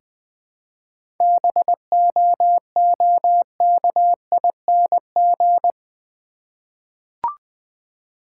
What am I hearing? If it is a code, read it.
Morse code, heard as a keyed tone, about 20 wpm, 690 Hz. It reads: BOOKING
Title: 7 Letter Words - First 3 Letters Given - 20wpm